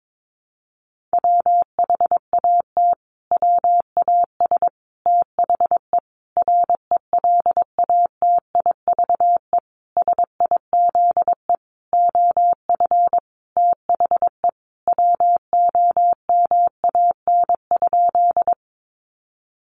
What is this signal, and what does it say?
Morse code, heard as a keyed tone, about 22 wpm, 700 Hz. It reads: WHAT WAS THE RELATIVE SIZE OF THE WOMAN?
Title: Sentences from Top 1400 Words 22wpm